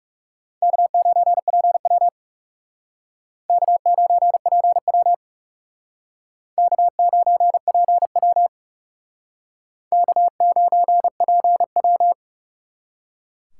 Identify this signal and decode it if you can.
Morse code, heard as a keyed tone, about 39 wpm, 700 Hz. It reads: K9PW K9PW K9PW K9PW